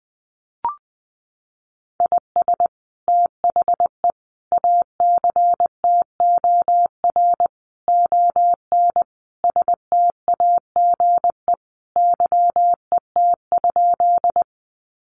Morse code, heard as a keyed tone, about 20 wpm, 700 Hz. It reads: IS THE ACTOR ON STAGE YET?
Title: Sentences from Top 1500 Words 20wpm